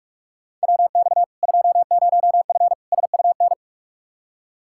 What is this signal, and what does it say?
Morse code, heard as a keyed tone, about 46 wpm, 700 Hz. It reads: WX 20F SUN
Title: Practice QSO - Exchange 2 - Info - 45wpm